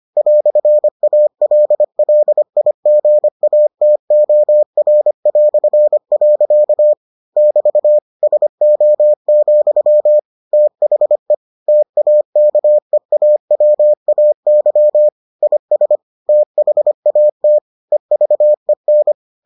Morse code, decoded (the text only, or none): "ALLIGATOR". <BT> SO, THE TAKEAWAY IS THAT EVEN